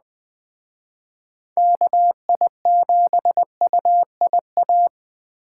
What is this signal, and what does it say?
Morse code, heard as a keyed tone, about 21 wpm, 700 Hz. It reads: KI7UIA